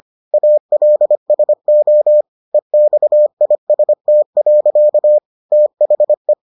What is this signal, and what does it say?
Morse code, heard as a keyed tone, about 25 wpm, 600 Hz. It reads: ALSO EXIST. THE